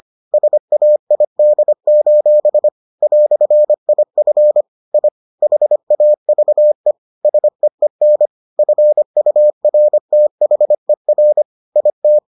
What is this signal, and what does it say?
Morse code, heard as a keyed tone, about 25 wpm, 600 Hz. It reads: SAID: "IF I HAVE SEEN FURTHER IT